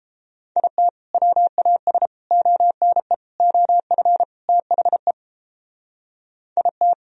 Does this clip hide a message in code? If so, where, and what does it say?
Yes: Morse code from 0.6 to 6.9 s, heard as a keyed tone, about 30 wpm, 700 Hz. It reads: IT WAS ONE OF THE IT